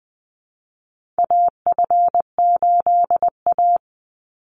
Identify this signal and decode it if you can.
Morse code, heard as a keyed tone, about 19 wpm, 700 Hz. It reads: AF8A
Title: Call Signs - 20wpm